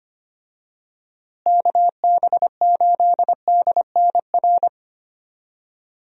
Morse code, heard as a keyed tone, about 25 wpm, 700 Hz. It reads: KB8DNR
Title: USA Call Signs - 6 Characters - Repeated 2x - 25wpm